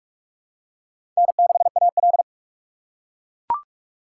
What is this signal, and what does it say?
Morse code, heard as a keyed tone, about 42 wpm, 700 Hz. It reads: N6AL
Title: USA Call Signs - 4 Characters - 6x Character Spacing - 45wpm